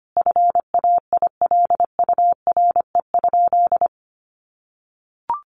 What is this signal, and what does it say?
Morse code, heard as a keyed tone, about 25 wpm, 700 Hz. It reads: FAILURE?
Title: Sentences from Top 1200 Words 25wpm